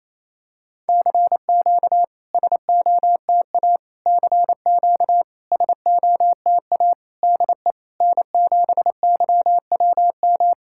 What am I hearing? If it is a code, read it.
Morse code, heard as a keyed tone, about 28 wpm, 700 Hz. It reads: CQ SOTA CQ SOTA DE N7YWM
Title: SOTA QSO Practice 28wpm